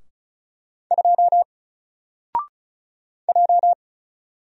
Morse code, heard as a keyed tone, about 33 wpm, 700 Hz. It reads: 2 J
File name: Single-Letter-Number-35wpm